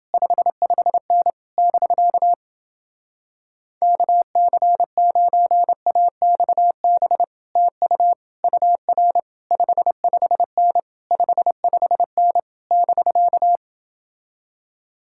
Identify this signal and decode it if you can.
Morse code, heard as a keyed tone, about 29 wpm, 700 Hz. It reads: 55N <BK> KC9AXB TU UR 55N 55N <BK>